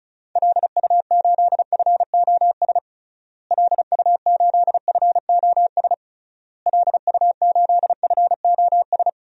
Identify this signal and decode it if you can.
Morse code, heard as a keyed tone, about 36 wpm, 700 Hz. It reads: LU8FOS LU8FOS LU8FOS
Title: DXCC-100 Call Signs - 6 Characters - Repeated 3x - 35wpm